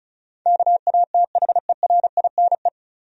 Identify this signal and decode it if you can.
Morse code, heard as a keyed tone, about 34 wpm, 700 Hz. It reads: KATHERINE
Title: USA Common Names - 35wpm